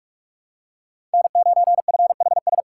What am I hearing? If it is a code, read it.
Morse code, heard as a keyed tone, about 46 wpm, 700 Hz. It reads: N9FHS
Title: USA Call Signs - 5 Characters - 2x Character Spacing - 45wpm